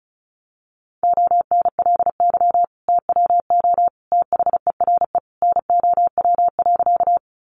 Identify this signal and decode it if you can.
Morse code, heard as a keyed tone, about 34 wpm, 700 Hz. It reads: ONLY TWO THERE NOW.
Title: Sentences from Top 200 Words 35wpm